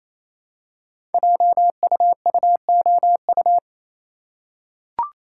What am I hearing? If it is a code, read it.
Morse code, heard as a keyed tone, about 27 wpm, 700 Hz. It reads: JUUOU
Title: Code Groups - 5 Letters - 28wpm